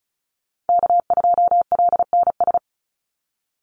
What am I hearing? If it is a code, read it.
Morse code, heard as a keyed tone, about 34 wpm, 700 Hz. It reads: K2LNS
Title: Call Signs Repeated 3x - 35wpm